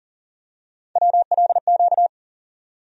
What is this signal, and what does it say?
Morse code, heard as a keyed tone, about 40 wpm, 700 Hz. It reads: WLQ
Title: USA Call Sign Postfixes - 3 Letters 40wpm